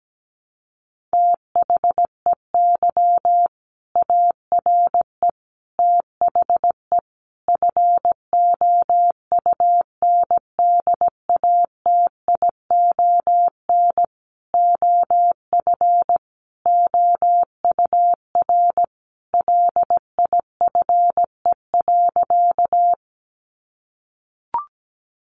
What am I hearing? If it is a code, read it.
Morse code, heard as a keyed tone, about 17 wpm, 700 Hz. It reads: THEY ARE THE FOUNDATION OF OUR LIFE.
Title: Sentences from Top 1200 Words 17wpm